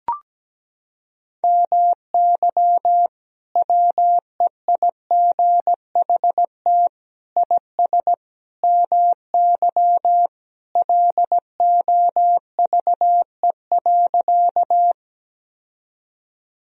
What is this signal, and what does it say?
Morse code, heard as a keyed tone, about 17 wpm, 700 Hz. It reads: MY WEIGHT IS MY LOVE.
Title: Sentences from Top 1000 Words - Review Part 3 - 17wpm